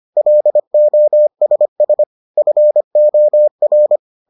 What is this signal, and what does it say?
Morse code, heard as a keyed tone, about 25 wpm, 600 Hz. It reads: LOSS FOR